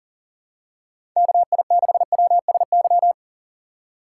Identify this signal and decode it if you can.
Morse code, heard as a keyed tone, about 38 wpm, 700 Hz. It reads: KI6WSY
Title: USA Call Signs - 6 Characters - 8x Character Spacing - 40wpm